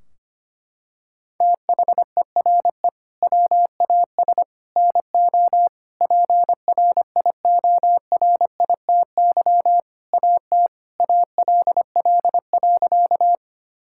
Morse code, heard as a keyed tone, about 25 wpm, 700 Hz. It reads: THERE WAS NO PRIORITY AT ALL.
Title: Sentences from Top 1600 Words 25wpm